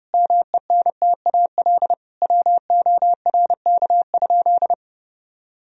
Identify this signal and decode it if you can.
Morse code, heard as a keyed tone, about 32 wpm, 700 Hz. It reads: MENTAL WORK?
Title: Sentences from Top 1000 Words - Review Part 4 - 30wpm